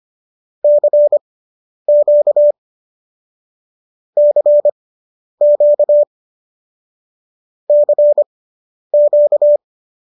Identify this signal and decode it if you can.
Morse code, heard as a keyed tone, about 25 wpm, spaced as for 10 wpm, 590 Hz. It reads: CQ CQ CQ